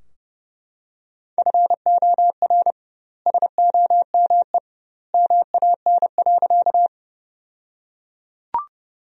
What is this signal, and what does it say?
Morse code, heard as a keyed tone, about 32 wpm, 700 Hz. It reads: FOR SOME MAN.
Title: 2x Word-Spacing - Sentences from Top 200 Words - 30wpm